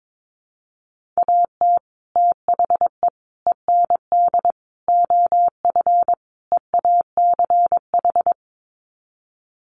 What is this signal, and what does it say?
Morse code, heard as a keyed tone, about 22 wpm, 700 Hz. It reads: AT THE END OF EACH
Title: Sets of 5 Words Encore 22wpm